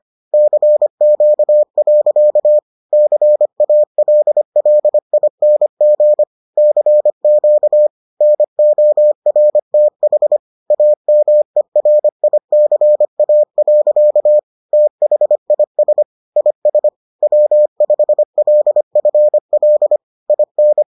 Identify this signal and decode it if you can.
Morse code, heard as a keyed tone, about 25 wpm, 600 Hz. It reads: CQ. CALLING CQ NORTH AMERICA. THIS IS W5LFL IN